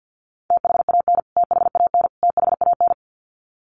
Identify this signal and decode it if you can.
Morse code, heard as a keyed tone, about 50 wpm, 700 Hz. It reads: THAN THAN THAN
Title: Top 500 Words Repeated 50wpm